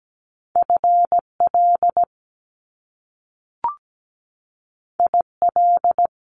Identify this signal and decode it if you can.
Morse code, heard as a keyed tone, about 17 wpm, 700 Hz. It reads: FL IL
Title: Top Two Letter Combinations in Words 17wpm